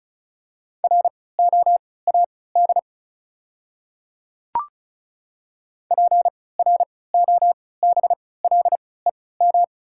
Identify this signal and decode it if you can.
Morse code, heard as a keyed tone, about 36 wpm, spaced as for 20 wpm, 690 Hz. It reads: ROAD PROBLEM
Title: 3x Character Spacing - Top 500 Words - 35wpm